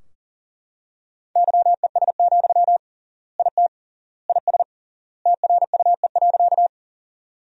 Morse code, heard as a keyed tone, about 40 wpm, 700 Hz. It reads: YES, IT IS TRUE.
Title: 3x Word-Spacing - Sentences from Top 500 Words - 40wpm